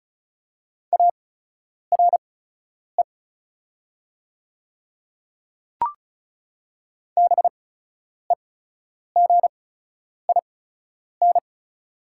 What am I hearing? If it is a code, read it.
Morse code, heard as a keyed tone, about 34 wpm, spaced as for 10 wpm, 700 Hz. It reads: ARE BEGIN
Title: 8x Character Spacing - Top 300 Words - 35wpm